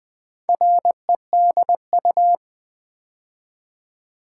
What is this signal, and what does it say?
Morse code, heard as a keyed tone, about 20 wpm, 700 Hz. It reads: REDU
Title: Build A Word - 2 Letters At A Time - 6-Letter Words - 20wpm